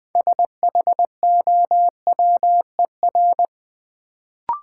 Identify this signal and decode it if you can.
Morse code, heard as a keyed tone, about 20 wpm, 700 Hz. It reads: SHOWER
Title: Sentences from Top 1600 Words - New Word Drill Alt - 20wpm